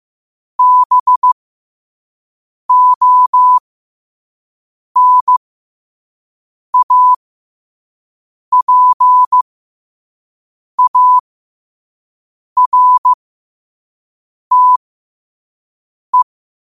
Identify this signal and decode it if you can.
Morse code, heard as a keyed tone, about 15 wpm, spaced as for 5 wpm, 1000 Hz. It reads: BONAPARTE
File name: qotdfiles / 5wpm